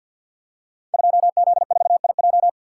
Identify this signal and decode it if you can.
Morse code, heard as a keyed tone, about 53 wpm, 700 Hz. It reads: 2Z4IJ
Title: Code Groups - 5 Letter:Numbers - 50wpm